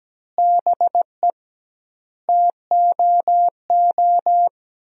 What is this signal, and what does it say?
Morse code, heard as a keyed tone, about 17 wpm, 700 Hz. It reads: BE TOO
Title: 2x Word-Spacing - Sentences from Top 300 Words - 17wpm